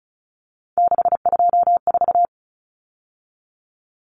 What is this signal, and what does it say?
Morse code, heard as a keyed tone, about 34 wpm, 700 Hz. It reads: 624
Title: Numbers - 3-Digit 35wpm